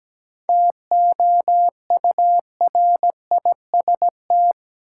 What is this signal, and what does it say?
Morse code, heard as a keyed tone, about 17 wpm, 700 Hz. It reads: TOURIST
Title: Seven Letter Words - 17wpm